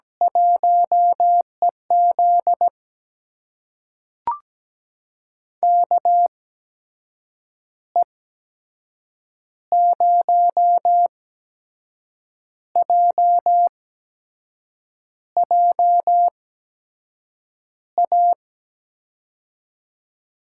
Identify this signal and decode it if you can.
Morse code, heard as a keyed tone, about 17 wpm, spaced as for 6 wpm, 700 Hz. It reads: 1EZ KE0JJA
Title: USA Call Signs - 8x Character Spacing - 17wpm